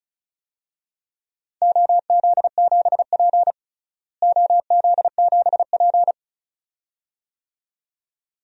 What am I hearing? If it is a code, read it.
Morse code, heard as a keyed tone, about 36 wpm, 700 Hz. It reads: OZ7P OZ7P
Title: DXCC-100 Call Signs - 4 Characters - Repeated 2x - 35wpm